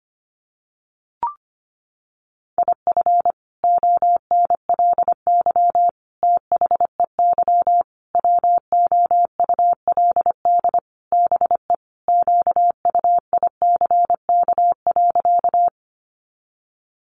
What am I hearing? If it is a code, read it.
Morse code, heard as a keyed tone, about 25 wpm, 700 Hz. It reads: IF ONLY THEY WOULD BE QUICK.
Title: Sentences from Top 500 Words 25wpm